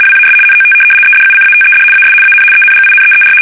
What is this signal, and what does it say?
RTTY, about 75 baud. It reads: WAS NOT EDUCATED AS AN ENGINEER
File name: fsk850-75B / teletype-corp